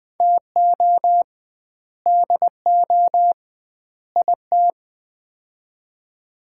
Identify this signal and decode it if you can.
Morse code, heard as a keyed tone, about 20 wpm, 700 Hz. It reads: TO DO IT
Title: 2x Word-Spacing - Sets of 3 Words - 20wpm